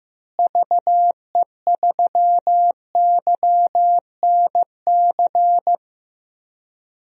Morse code, heard as a keyed tone, about 15 wpm, 700 Hz. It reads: VE3YNC
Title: DXCC-100 Call Signs - 6 Characters - Repeated 2x - 15wpm